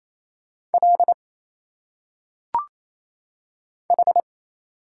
Morse code, heard as a keyed tone, about 27 wpm, 700 Hz. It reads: L H
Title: Single Letters - Phonetic Pronunciation 28wpm